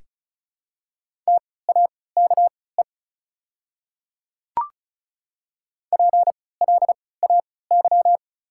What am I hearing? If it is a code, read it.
Morse code, heard as a keyed tone, about 34 wpm, spaced as for 20 wpm, 690 Hz. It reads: TAKE PLAY
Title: 3x Character Spacing - Top 200 Words - 35wpm